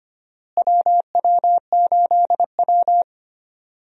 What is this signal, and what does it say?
Morse code, heard as a keyed tone, about 25 wpm, 700 Hz. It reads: WW8W
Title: USA Call Signs - 4 Characters - Repeated 3x - 25wpm